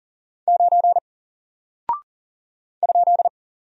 Morse code, heard as a keyed tone, about 40 wpm, 700 Hz. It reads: 9 ?
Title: Single Character (Letter:Number:Punctuation:Prosign) Alt 40wpm